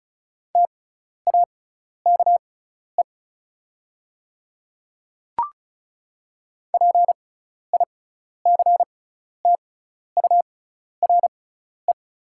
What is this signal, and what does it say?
Morse code, heard as a keyed tone, about 36 wpm, spaced as for 12 wpm, 690 Hz. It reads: TAKE PICTURE
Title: 6x Character Spacing - Top 300 Words - 35wpm